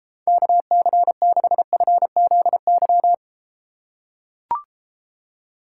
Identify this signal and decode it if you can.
Morse code, heard as a keyed tone, about 33 wpm, 700 Hz. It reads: KC6FZY